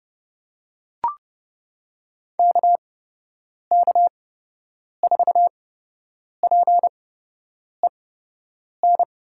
Morse code, heard as a keyed tone, about 30 wpm, spaced as for 8 wpm, 700 Hz. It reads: KK4PEN
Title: USA Call Signs - 8x Character Spacing - 30wpm